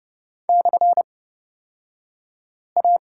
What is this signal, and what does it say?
Morse code, heard as a keyed tone, about 30 wpm, spaced as for 5 wpm, 700 Hz. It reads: /A
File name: Lesson 38d - Letter q - Two Characters 30wpm